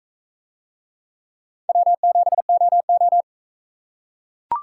Morse code, heard as a keyed tone, about 43 wpm, 700 Hz. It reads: W7OO